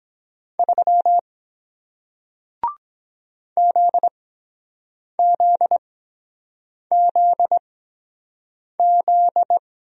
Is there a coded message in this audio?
Yes: Morse code from 0.6 to 9.6 s, heard as a keyed tone, about 23 wpm, spaced as for 7 wpm, 700 Hz. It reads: 3 ZZZZ